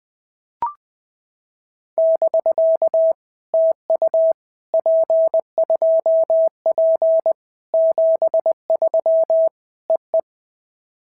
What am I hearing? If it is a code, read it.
Morse code, heard as a keyed tone, about 20 wpm, 650 Hz. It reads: <BK> TU P2P 73 EE
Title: POTA QSO Practice - 20wpm